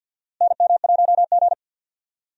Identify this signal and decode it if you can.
Morse code, heard as a keyed tone, about 52 wpm, 700 Hz. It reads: NM1G
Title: USA Call Signs - 4 Characters - Repeated 2x - 50wpm